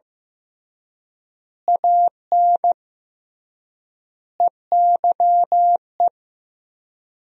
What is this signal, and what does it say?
Morse code, heard as a keyed tone, about 15 wpm, 700 Hz. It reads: AN EYE
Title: 3x Word-Spacing - Sentences from Top 300 Words - 15wpm